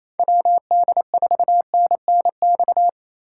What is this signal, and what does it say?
Morse code, heard as a keyed tone, about 28 wpm, 690 Hz. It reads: WD4NNX
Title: USA Call Signs - 3x Character Spacing - 28wpm